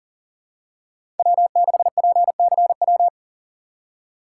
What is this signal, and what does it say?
Morse code, heard as a keyed tone, about 38 wpm, 700 Hz. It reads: W6PCW
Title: USA Call Signs - 5 Characters - 3x Character Spacing - 40wpm